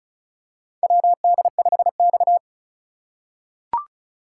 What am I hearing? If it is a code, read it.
Morse code, heard as a keyed tone, about 36 wpm, 700 Hz. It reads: WD5X
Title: USA Call Signs - 4 Characters - 4x Character Spacing - 35wpm